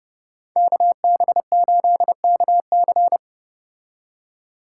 Morse code, heard as a keyed tone, about 30 wpm, 700 Hz. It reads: KB8KC